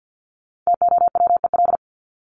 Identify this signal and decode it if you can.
Morse code, heard as a keyed tone, about 48 wpm, 700 Hz. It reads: TOWEL
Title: Top 1501-1750 Words 50wpm